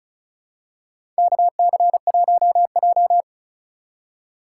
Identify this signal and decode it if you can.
Morse code, heard as a keyed tone, about 36 wpm, 700 Hz. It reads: KC1J
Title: USA Call Signs - 4 Characters - 3x Character Spacing - 35wpm